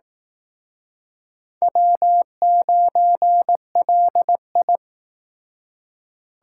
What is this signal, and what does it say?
Morse code, heard as a keyed tone, about 18 wpm, 700 Hz. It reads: W9LI
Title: Speed Racing - USA 1x2 Call Signs 15wpm